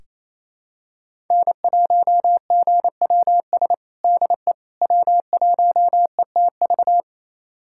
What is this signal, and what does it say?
Morse code, heard as a keyed tone, about 28 wpm, 700 Hz. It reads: N1GWS DE W1ETV
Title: USA Callsign DE Callsign - 5 Characters - 28wpm